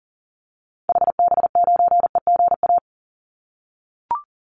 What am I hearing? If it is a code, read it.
Morse code, heard as a keyed tone, about 40 wpm, 700 Hz. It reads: HB9EGA
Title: Call Signs - 40wpm